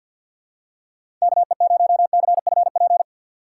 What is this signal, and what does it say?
Morse code, heard as a keyed tone, about 53 wpm, 700 Hz. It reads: KE0CFP
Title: USA Call Signs - 6 Characters - 6x Character Spacing - 50wpm